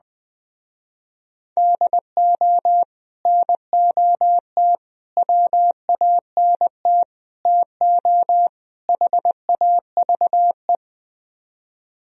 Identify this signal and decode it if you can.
Morse code, heard as a keyed tone, about 21 wpm, 700 Hz. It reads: DO NOT WANT TO HAVE
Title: Speed-Racing - Sets of 5 Words 20wpm